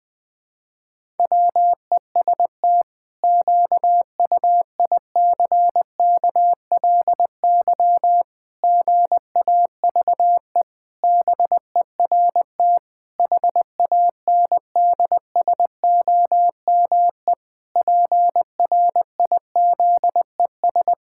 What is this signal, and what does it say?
Morse code, heard as a keyed tone, about 21 wpm, 700 Hz. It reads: WEST QUICKLY GAVE BERT HANDSOME PRIZES
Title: Pangram Challenge v2 - 20wpm